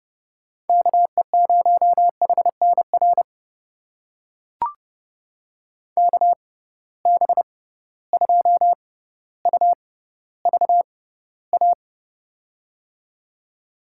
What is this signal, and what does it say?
Morse code, heard as a keyed tone, about 30 wpm, spaced as for 15 wpm, 700 Hz. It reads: KE0HNR KB2UVA
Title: USA Call Signs - 6 Characters - 6x Character Spacing - 30wpm